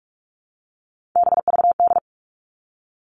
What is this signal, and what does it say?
Morse code, heard as a keyed tone, about 46 wpm, 700 Hz. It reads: BVD
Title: Three Challenging Letter-Number Combinations 45wpm